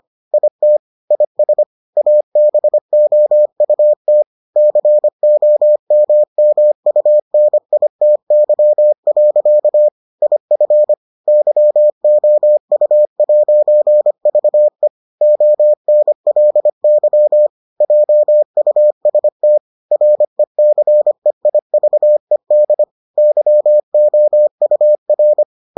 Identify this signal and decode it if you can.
Morse code, heard as a keyed tone, about 25 wpm, 600 Hz. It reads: IT IS ABOUT COMMUNITY. IF YOU'VE ONLY JUST RECEIVED YOUR